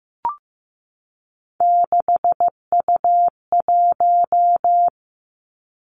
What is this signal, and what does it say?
Morse code, heard as a keyed tone, about 15 wpm, 700 Hz. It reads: 6U1
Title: Three Challenging Letter-Number Combinations 15wpm